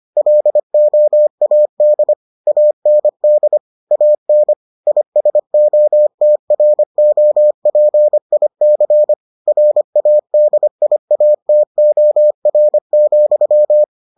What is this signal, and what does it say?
Morse code, heard as a keyed tone, about 25 wpm, 600 Hz. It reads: LOAD AND AN ISOTROPIC RADIATOR,